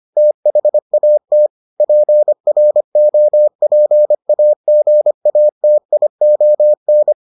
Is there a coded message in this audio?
Yes: Morse code from 0.2 to 7.1 s, heard as a keyed tone, about 25 wpm, 600 Hz. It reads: THAT PROPAGATION